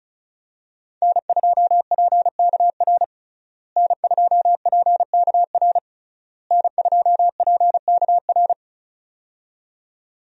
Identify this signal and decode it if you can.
Morse code, heard as a keyed tone, about 34 wpm, 700 Hz. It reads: N2PKR N2PKR N2PKR
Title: USA Call Signs - 5 Characters - Repeated 3x - 35wpm